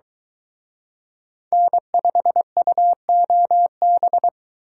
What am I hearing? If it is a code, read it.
Morse code, heard as a keyed tone, about 23 wpm, 700 Hz. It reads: N5UOB